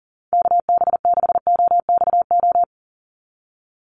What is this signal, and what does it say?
Morse code, heard as a keyed tone, about 40 wpm, 700 Hz. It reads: KB6OXO
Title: USA Call Signs 40wpm